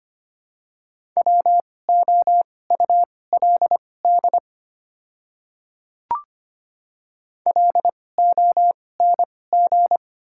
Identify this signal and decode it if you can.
Morse code, heard as a keyed tone, about 25 wpm, spaced as for 18 wpm, 690 Hz. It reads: WOULD LONG
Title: 2x Character Spacing - Top 100 Words - 25wpm